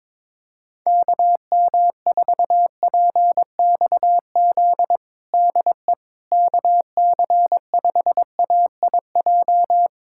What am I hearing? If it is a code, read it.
Morse code, heard as a keyed tone, about 22 wpm, 700 Hz. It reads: KM4PXZ DE KC5AIJ
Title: USA Callsign DE Callsign - 6 Characters - 22wpm